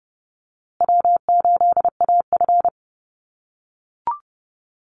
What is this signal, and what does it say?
Morse code, heard as a keyed tone, about 30 wpm, 700 Hz. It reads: W8AF
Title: USA Call Signs Repeated 2x 30wpm